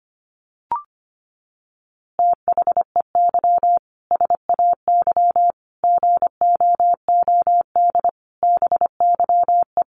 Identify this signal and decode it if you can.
Morse code, heard as a keyed tone, about 25 wpm, 700 Hz. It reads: THEY SAY GOOD BYE
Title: Easy Phrases 25wpm